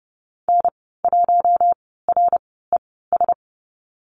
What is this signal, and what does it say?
Morse code, heard as a keyed tone, about 30 wpm, spaced as for 17 wpm, 700 Hz. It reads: N1RES
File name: Lesson 14g - Number 2 - Call Signs 30wpm